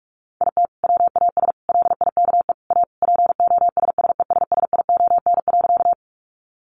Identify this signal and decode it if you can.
Morse code, heard as a keyed tone, about 44 wpm, 700 Hz. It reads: IT WAS LIKE A POSSESSION.
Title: Sentences from Top 1400 Words 45wpm